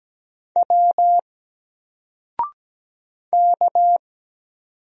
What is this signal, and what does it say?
Morse code, heard as a keyed tone, about 17 wpm, 700 Hz. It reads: W K
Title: Single Character (Letter:Number:Punctuation:Prosign) Alt 17wpm